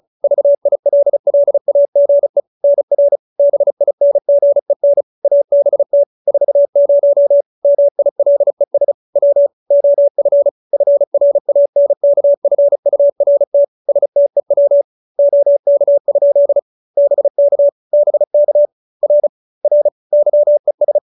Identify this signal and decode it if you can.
Morse code, heard as a keyed tone, about 34 wpm, 570 Hz. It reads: VILLAGE NR BINGEN ABT 40 MILES W OF FRANKFURT STEW OK? BK BK R R YES